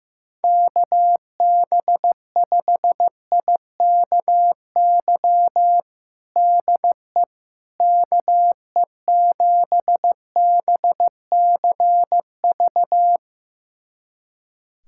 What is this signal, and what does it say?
Morse code, heard as a keyed tone, about 15 wpm, 700 Hz. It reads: KB5IKY DE KE7BCV
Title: USA Callsign DE Callsign - 6 Characters - 15wpm